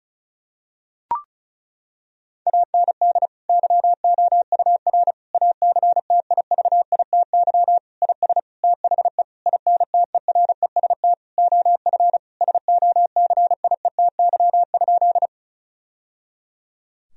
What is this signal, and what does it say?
Morse code, heard as a keyed tone, about 36 wpm, 700 Hz. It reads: AND YOUR ACTIVITY IS THE INTEREST OF SOCIETY?
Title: Sentences from Top 1000 Words - Review Part 4 - 35wpm